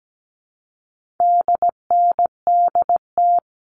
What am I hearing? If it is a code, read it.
Morse code, heard as a keyed tone, about 17 wpm, 700 Hz. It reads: DNDT
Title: All Cut Numbers - 4-Digit 17wpm